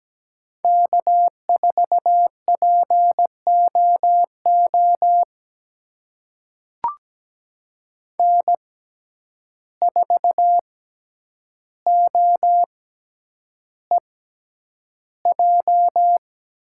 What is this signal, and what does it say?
Morse code, heard as a keyed tone, about 17 wpm, spaced as for 9 wpm, 700 Hz. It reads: K4POO N4OEJ
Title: USA Call Signs - 5 Characters - 6x Character Spacing - 17wpm